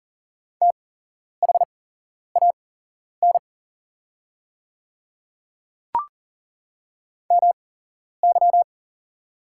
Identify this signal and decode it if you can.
Morse code, heard as a keyed tone, about 38 wpm, spaced as for 11 wpm, 690 Hz. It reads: THAN MY
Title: 8x Character Spacing - Top 200 Words - 40wpm